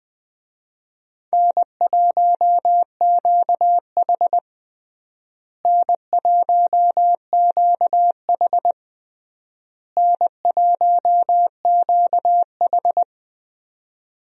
Morse code, heard as a keyed tone, about 21 wpm, 700 Hz. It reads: N1QH N1QH N1QH
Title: USA Call Signs - 4 Characters - Repeated 3x - 20wpm